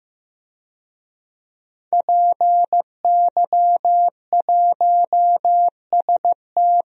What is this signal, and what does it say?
Morse code, heard as a keyed tone, about 16 wpm, 700 Hz. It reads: PY1ST
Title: DXCC-100 Call Signs - 5 Characters - 15wpm